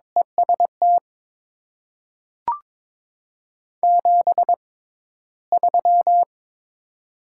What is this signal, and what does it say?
Morse code, heard as a keyed tone, about 22 wpm, spaced as for 13 wpm, 700 Hz. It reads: EST 73
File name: Top 90 QSO Elements - 6x Character Spacing - 22wpm